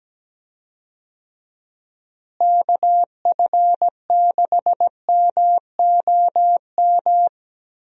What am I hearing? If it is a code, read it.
Morse code, heard as a keyed tone, about 17 wpm, 700 Hz. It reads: KF6MOM
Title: USA Call Signs - 6 Characters - Repeated 3x - 17wpm